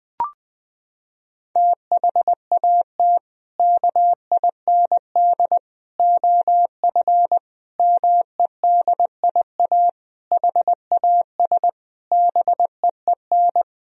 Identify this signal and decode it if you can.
Morse code, heard as a keyed tone, about 20 wpm, 700 Hz. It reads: THAT KIND OF MEDIA HAS BEEN
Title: Sentences from Top 1000 Words - Review Part 4 - 20wpm